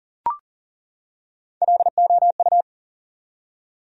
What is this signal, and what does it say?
Morse code, heard as a keyed tone, about 40 wpm, 700 Hz. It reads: LOU
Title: USA Common Names - Max 3 Letters - 40wpm